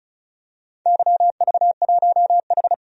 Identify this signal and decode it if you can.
Morse code, heard as a keyed tone, about 36 wpm, 690 Hz. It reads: YV1H
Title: DXCC-100 Call Signs - 4 Characters - 35wpm